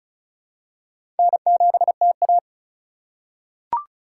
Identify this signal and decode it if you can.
Morse code, heard as a keyed tone, about 36 wpm, 700 Hz. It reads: N7TA
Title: USA Call Signs - 4x Character Spacing - 35wpm